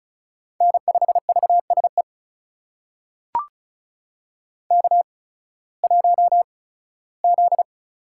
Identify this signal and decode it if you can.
Morse code, heard as a keyed tone, about 34 wpm, spaced as for 18 wpm, 700 Hz. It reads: N5VSE K1Z
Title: USA Call Signs - 5 Characters - 8x Character Spacing - 35wpm